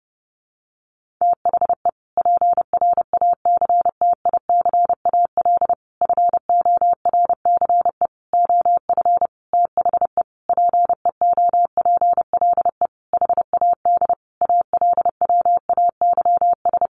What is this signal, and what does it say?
Morse code, heard as a keyed tone, about 30 wpm, 700 Hz. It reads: THE PRACTICAL FORCE OF THE PEOPLE HAD ALWAYS
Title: Sentences from Top 1400 Words 30wpm